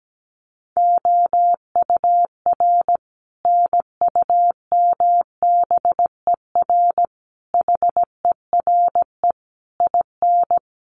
Morse code, heard as a keyed tone, about 17 wpm, 700 Hz. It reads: OUR NUMBER HERE IN
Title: Sets of 5 Words Encore2 17wpm